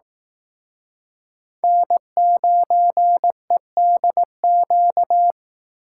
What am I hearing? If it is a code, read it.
Morse code, heard as a keyed tone, about 18 wpm, 700 Hz. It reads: N9EDQ
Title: Speed Racing - USA 1x3 Call Signs 15wpm